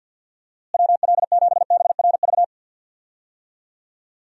Morse code, heard as a keyed tone, about 50 wpm, 700 Hz. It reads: WL7BRV
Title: USA Call Signs - 6 Characters - Repeated 2x - 50wpm